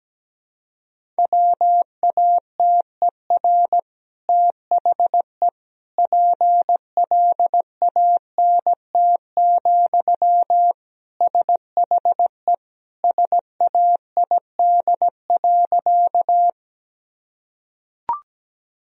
Morse code, heard as a keyed tone, about 17 wpm, 700 Hz. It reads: WATER THE PLANT, SHE SAID.
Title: Sentences from Top 1000 Words - Review Part 1 - 17wpm